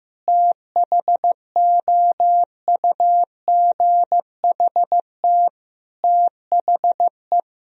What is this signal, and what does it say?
Morse code, heard as a keyed tone, about 15 wpm, 700 Hz. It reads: THOUGHT THE
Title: Sentences from Top 1000 Words - Review Part 1 - 15wpm